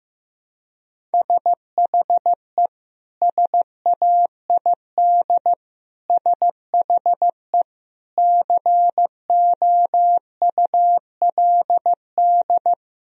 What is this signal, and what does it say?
Morse code, heard as a keyed tone, about 15 wpm, 700 Hz. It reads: SHE SAID SHE COULD
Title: Sentences from Top 1000 Words - Review Part 4 - 15wpm